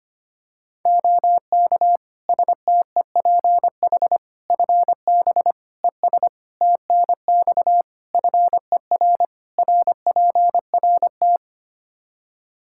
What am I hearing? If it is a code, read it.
Morse code, heard as a keyed tone, about 25 wpm, 700 Hz. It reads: OK STEPH FB ES TNX FER RPRT
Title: Practice QSO - Exchange 2 - Info - 25wpm